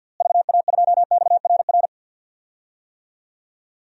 Morse code, heard as a keyed tone, about 53 wpm, 700 Hz. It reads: VA2XRR
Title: DXCC-100 Call Signs - 6 Characters - Repeated 3x - 50wpm